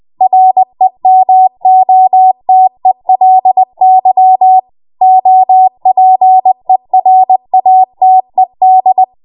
Morse code, heard as a keyed tone, about 20 wpm, 750 Hz. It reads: REMOTELY OPERATED
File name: Morse / 20WPM